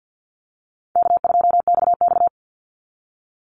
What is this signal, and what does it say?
Morse code, heard as a keyed tone, about 50 wpm, 700 Hz. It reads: K2XX
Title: Easy Call Signs - 50wpm